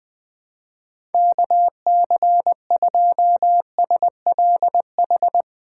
Morse code, heard as a keyed tone, about 20 wpm, 700 Hz. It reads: KC2SLH
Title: USA Call Signs - 8x Character Spacing - 20wpm